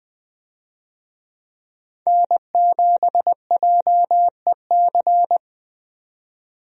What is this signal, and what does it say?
Morse code, heard as a keyed tone, about 20 wpm, 700 Hz. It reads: N7JEC
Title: USA Call Signs - 5 Characters - 20wpm